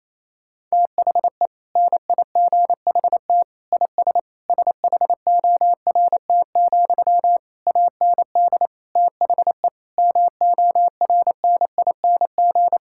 Morse code, heard as a keyed tone, about 28 wpm, 700 Hz. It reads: THE NIGHT IS SHORT, AND THE MORNING
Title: Sentences from Top 1600 Words 28wpm